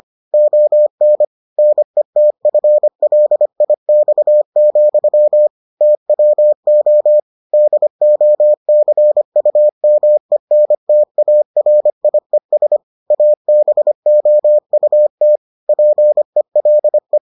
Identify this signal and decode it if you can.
Morse code, heard as a keyed tone, about 25 wpm, 600 Hz. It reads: ON NETFLIX, TWO DOCUMENTARIES ABOUT PELE